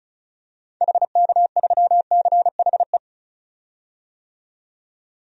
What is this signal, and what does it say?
Morse code, heard as a keyed tone, about 36 wpm, 700 Hz. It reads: HK3CHE
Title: DXCC-100 Call Signs - 6 Characters - 35wpm